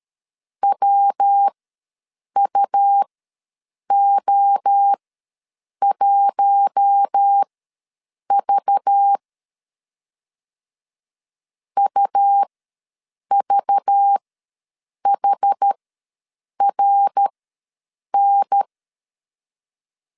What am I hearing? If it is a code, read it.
Morse code, heard as a keyed tone, about 13 wpm, spaced as for 7 wpm, 800 Hz. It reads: WUO1V UVHRN
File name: LEZIONI / 5 LBVC / 5 LBV ws 12wpm 3015